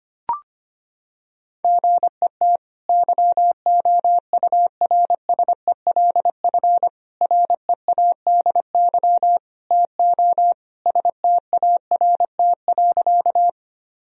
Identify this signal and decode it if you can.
Morse code, heard as a keyed tone, about 25 wpm, 700 Hz. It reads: GET YOURSELF READY TO START.
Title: Sentences from Top 1000 Words - Review Part 1 - 25wpm